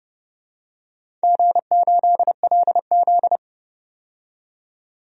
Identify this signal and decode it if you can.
Morse code, heard as a keyed tone, about 30 wpm, 700 Hz. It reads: G8LZ
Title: DXCC-100 Call Signs - 4 Characters - 30wpm